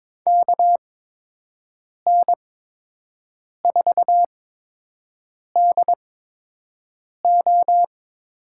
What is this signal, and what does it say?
Morse code, heard as a keyed tone, about 22 wpm, spaced as for 6 wpm, 700 Hz. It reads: KN4DO
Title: USA Call Signs - 8x Character Spacing - 22wpm